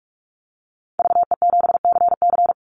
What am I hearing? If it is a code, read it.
Morse code, heard as a keyed tone, about 44 wpm, 700 Hz. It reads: VE7CC
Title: Call Signs - 45wpm